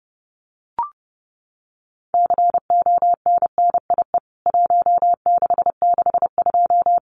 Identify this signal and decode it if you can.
Morse code, heard as a keyed tone, about 30 wpm, 700 Hz. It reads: CONNIE 1662
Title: CWT Exchange 30wpm